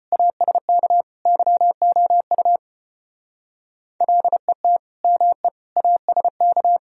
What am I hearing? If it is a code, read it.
Morse code, heard as a keyed tone, about 31 wpm, 700 Hz. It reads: ASK YOU LET ME ASK
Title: Speed-Racing - Sets of 4 Words 25wpm